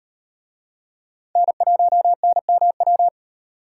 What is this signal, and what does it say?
Morse code, heard as a keyed tone, about 37 wpm, 700 Hz. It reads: N1NMW